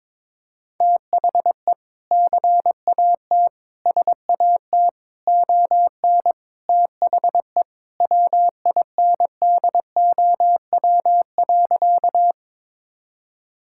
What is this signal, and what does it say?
Morse code, heard as a keyed tone, about 22 wpm, 700 Hz. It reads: THE CAT SAT ON THE WINDOW.
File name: practice / 22wpm / Sentences with 2-Word Phrases - 22wpm